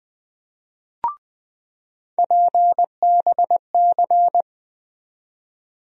Morse code, heard as a keyed tone, about 20 wpm, 700 Hz. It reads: PBC
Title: USA Call Sign Postfixes - 3 Letters 20wpm